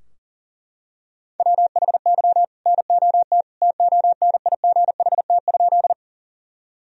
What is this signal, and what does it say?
Morse code, heard as a keyed tone, about 40 wpm, 700 Hz. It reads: WHY NOT TONIGHT?
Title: Sentences from Top 1000 Words - Review Part 3 - 40wpm